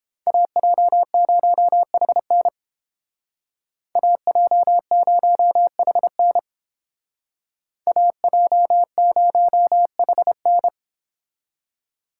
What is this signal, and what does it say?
Morse code, heard as a keyed tone, about 31 wpm, 700 Hz. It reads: AJ0HN AJ0HN AJ0HN